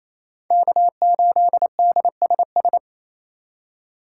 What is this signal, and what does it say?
Morse code, heard as a keyed tone, about 28 wpm, 700 Hz. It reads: K8DSS
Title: K1USN SST QSO Practice - 28wpm